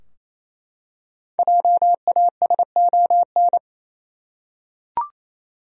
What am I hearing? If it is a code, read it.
Morse code, heard as a keyed tone, about 27 wpm, 700 Hz. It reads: JASON
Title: USA Common Names - Max 5 Letters - 28wpm